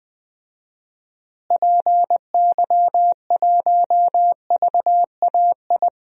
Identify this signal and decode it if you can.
Morse code, heard as a keyed tone, about 21 wpm, 700 Hz. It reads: PY1VAI
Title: DXCC-100 Call Signs - 6 Characters - Repeated 2x - 20wpm